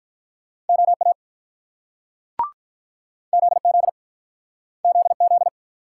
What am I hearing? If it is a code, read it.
Morse code, heard as a keyed tone, about 48 wpm, 700 Hz. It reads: OA ZZ ZZ